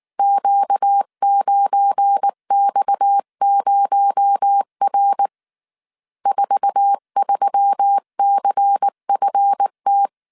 Morse code, heard as a keyed tone, about 19 wpm, 800 Hz. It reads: Q9X0R 43CFT
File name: CW / DET / 1 9 1500C S C 18wpm 2539